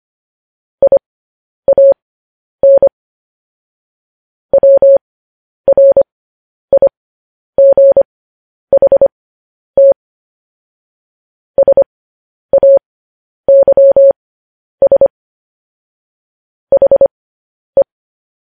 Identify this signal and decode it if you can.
Morse code, heard as a keyed tone, about 25 wpm, spaced as for 10 wpm, 550 Hz. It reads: IAN WRIGHT SAYS HE
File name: podcast / News headlines-1x-25x10